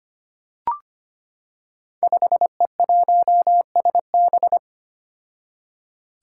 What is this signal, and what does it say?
Morse code, heard as a keyed tone, about 25 wpm, 700 Hz. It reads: 5E1SB
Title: Five Challenging Letter-Number Combinations 25wpm